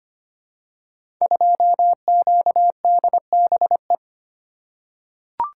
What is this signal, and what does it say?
Morse code, heard as a keyed tone, about 25 wpm, 700 Hz. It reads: 2QDBE
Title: Code Groups - 5 Letter:Numbers - 25wpm